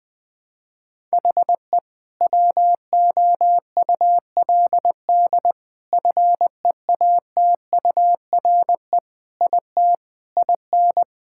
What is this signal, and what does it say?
Morse code, heard as a keyed tone, about 20 wpm, 700 Hz. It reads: HE WOULD FEATURE IT IN
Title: Sentences from Top 1000 Words - Review Part 5 - 20wpm